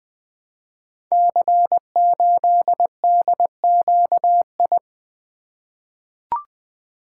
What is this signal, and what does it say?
Morse code, heard as a keyed tone, about 20 wpm, 700 Hz. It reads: C8DQI
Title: Code Groups - 5 Letter:Numbers - 20wpm